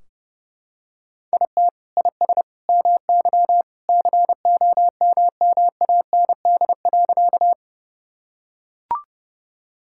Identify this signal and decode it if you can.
Morse code, heard as a keyed tone, about 30 wpm, 700 Hz. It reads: IT IS MY COMMAND.
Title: Sentences from Top 1600 Words 30wpm